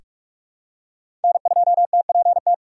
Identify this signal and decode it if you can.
Morse code, heard as a keyed tone, about 43 wpm, 700 Hz. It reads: N2TPT
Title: USA Call Signs - 4x Character Spacing - 45wpm